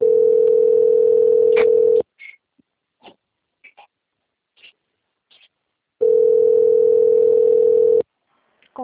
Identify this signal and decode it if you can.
Ringback tone (North America), 440 + 480 Hz.
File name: out- - - - - .13338